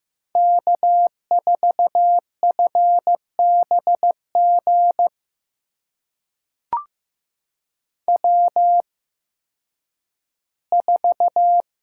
Morse code, heard as a keyed tone, about 15 wpm, spaced as for 10 wpm, 700 Hz. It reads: K4FBG W4
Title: USA Call Signs - 8x Character Spacing - 15wpm